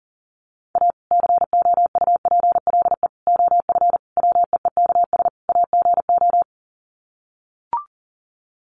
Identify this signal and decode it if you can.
Morse code, heard as a keyed tone, about 40 wpm, 700 Hz. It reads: A COUPLE OF WEEKS AGO
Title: Sets of 5 Words 40wpm